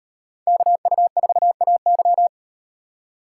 Morse code, heard as a keyed tone, about 36 wpm, 700 Hz. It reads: KU4AY